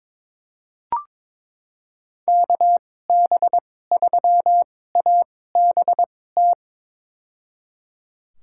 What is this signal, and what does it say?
Morse code, heard as a keyed tone, about 22 wpm, spaced as for 16 wpm, 700 Hz. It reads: KB3ABT
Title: USA Call Signs - 6 Characters - 2x Character Spacing - 22wpm